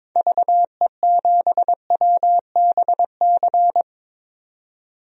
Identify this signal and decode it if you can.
Morse code, heard as a keyed tone, about 22 wpm, 700 Hz. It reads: VE7WBC
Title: DXCC-100 Call Signs - 6 Characters - 22wpm